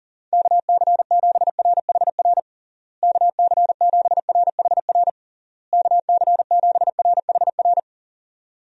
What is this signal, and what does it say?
Morse code, heard as a keyed tone, about 40 wpm, 700 Hz. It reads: KC7RHR KC7RHR KC7RHR
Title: USA Call Signs - 6 Characters - Repeated 3x - 40wpm